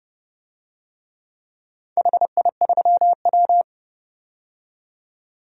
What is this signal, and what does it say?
Morse code, heard as a keyed tone, about 30 wpm, 700 Hz. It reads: HI3W
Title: DXCC-100 Call Signs - 4 Characters - 30wpm